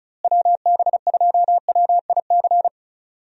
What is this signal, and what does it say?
Morse code, heard as a keyed tone, about 36 wpm, 700 Hz. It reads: WB2WIC
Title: USA Call Signs - 6 Characters - Repeated 3x - 35wpm